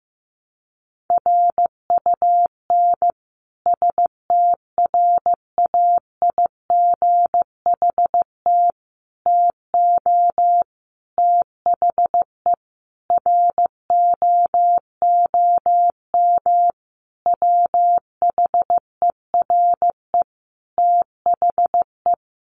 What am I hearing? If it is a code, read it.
Morse code, heard as a keyed tone, about 15 wpm, 700 Hz. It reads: RUN STRAIGHT TO THE ROOM WHERE THE
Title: Sentences from Top 1000 Words 15wpm